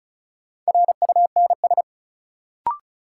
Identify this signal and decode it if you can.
Morse code, heard as a keyed tone, about 36 wpm, 700 Hz. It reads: RUNS
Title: Sentences from Top 600 Words - New Word Drill - 35wpm